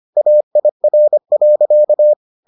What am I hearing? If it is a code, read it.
Morse code, heard as a keyed tone, about 25 wpm, 600 Hz. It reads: AIR.